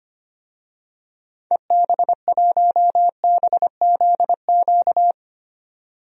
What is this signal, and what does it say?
Morse code, heard as a keyed tone, about 25 wpm, 700 Hz. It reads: EB1BZQ
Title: DXCC-100 Call Signs - 6 Characters - 25wpm